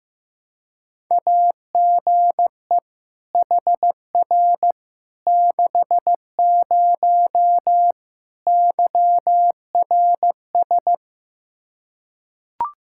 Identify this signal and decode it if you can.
Morse code, heard as a keyed tone, about 15 wpm, 700 Hz. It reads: AGE HR 60 YRS
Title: Practice QSO - Exchange 3 - More Info - 15wpm